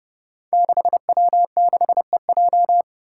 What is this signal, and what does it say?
Morse code, heard as a keyed tone, about 30 wpm, 700 Hz. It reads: 6W6EJ
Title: Five Challenging Letter-Number Combinations 30wpm